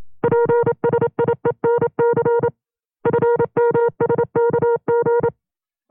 Morse code, heard as a keyed tone, about 27 wpm, 470 Hz. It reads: PSIENC FMSKG
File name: FreeFiles / CW-practice / 25WPM-side1